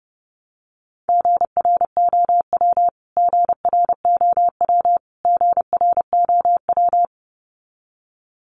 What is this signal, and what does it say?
Morse code, heard as a keyed tone, about 30 wpm, 700 Hz. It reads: GROW GROW GROW
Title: Top 500 Words Repeated 30wpm